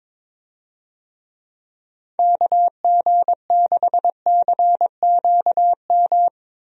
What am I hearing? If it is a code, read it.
Morse code, heard as a keyed tone, about 22 wpm, 700 Hz. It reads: KG6CQM
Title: USA Call Signs - 6 Characters - 22wpm